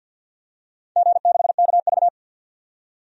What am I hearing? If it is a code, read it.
Morse code, heard as a keyed tone, about 50 wpm, 700 Hz. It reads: G6KV
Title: DXCC-100 Call Signs - 4 Characters - 50wpm